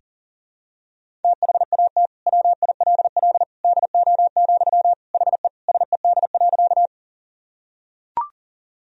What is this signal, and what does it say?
Morse code, heard as a keyed tone, about 40 wpm, 700 Hz. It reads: THAT WILL DO, HE SED.
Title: Sentences from Top 100 Words and Abbreviations 40wpm